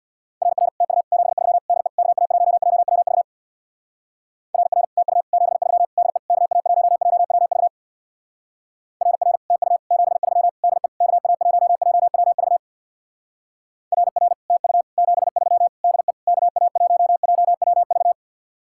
Morse code, heard as a keyed tone, about 62 wpm, 700 Hz. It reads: RR TU 73 DE KA1JWV RR TU 73 DE KA1JWV RR TU 73 DE KA1JWV RR TU 73 DE KA1JWV
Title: Speed-Racing Common QSO Phrases - 50wpm